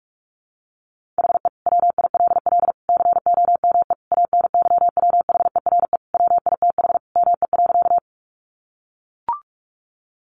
Morse code, heard as a keyed tone, about 44 wpm, 700 Hz. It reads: HE WILL COME ANYWHERE WITH ME.
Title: Sentences from Top 1400 Words 45wpm